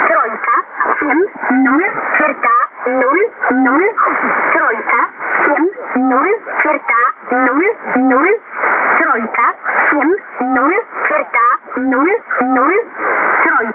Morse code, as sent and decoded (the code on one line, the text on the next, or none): none